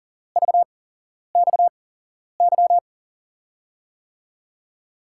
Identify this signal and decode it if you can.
Morse code, heard as a keyed tone, about 38 wpm, spaced as for 11 wpm, 700 Hz. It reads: VXY
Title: USA Call Signs - 8x Character Spacing - 40wpm